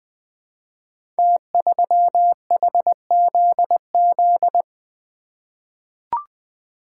Morse code, heard as a keyed tone, about 20 wpm, 700 Hz. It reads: T3HZZ
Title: Code Groups - 5 Letter:Numbers - 20wpm